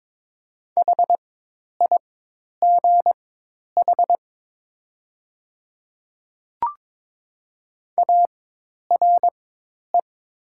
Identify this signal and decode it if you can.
Morse code, heard as a keyed tone, about 22 wpm, spaced as for 10 wpm, 700 Hz. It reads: HIGH ARE
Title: 4x Character Spacing - Top 200 Words - 22wpm